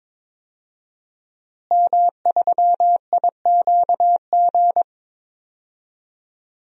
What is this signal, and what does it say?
Morse code, heard as a keyed tone, about 22 wpm, 700 Hz. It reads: M3IQG
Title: DXCC-100 Call Signs - 5 Characters - Repeated 2x - 22wpm